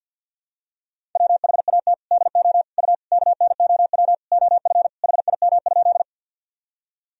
Morse code, heard as a keyed tone, about 47 wpm, 700 Hz. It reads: WHAT DO U KNOW OF HIM?
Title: Sentences from Top 100 Words and Abbreviations 50wpm